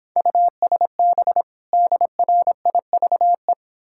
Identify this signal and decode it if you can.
Morse code, heard as a keyed tone, about 26 wpm, 700 Hz. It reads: USB DRIVE